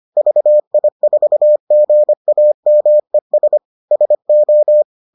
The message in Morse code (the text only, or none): VI4GAMES SO